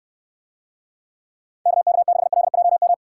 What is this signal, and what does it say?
Morse code, heard as a keyed tone, about 69 wpm, 700 Hz. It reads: KK6LJM